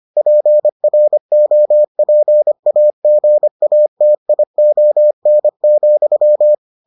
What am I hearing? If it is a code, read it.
Morse code, heard as a keyed tone, about 25 wpm, 600 Hz. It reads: PROPAGATION,